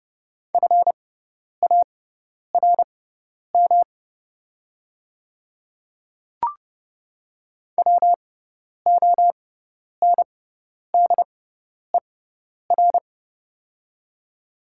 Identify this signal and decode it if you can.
Morse code, heard as a keyed tone, about 30 wpm, spaced as for 10 wpm, 690 Hz. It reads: FARM WONDER
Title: 6x Character Spacing - Top 500 Words - 30wpm